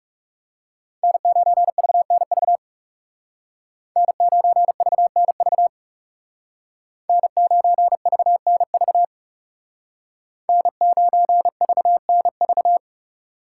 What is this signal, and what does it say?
Morse code, heard as a keyed tone, about 35 wpm, 700 Hz. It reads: N9VNV N9VNV N9VNV N9VNV
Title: Speed Racing - USA 1x3 Call Signs 30wpm